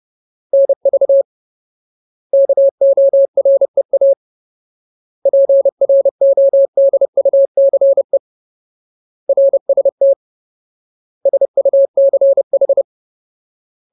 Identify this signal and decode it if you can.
Morse code, heard as a keyed tone, about 30 wpm, 550 Hz. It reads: NV KOREA PRODUCE RST SUCH
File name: Ultimate Word List / 30WPM Ultimate